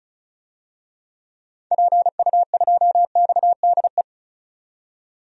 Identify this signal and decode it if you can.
Morse code, heard as a keyed tone, about 36 wpm, 700 Hz. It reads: PU2XDE
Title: DXCC-100 Call Signs - 6 Characters - 35wpm